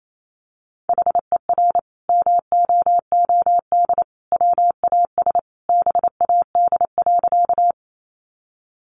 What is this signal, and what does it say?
Morse code, heard as a keyed tone, about 28 wpm, 700 Hz. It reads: HER MOOD WAS BAD.
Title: Sentences from Top 1100 Words 28wpm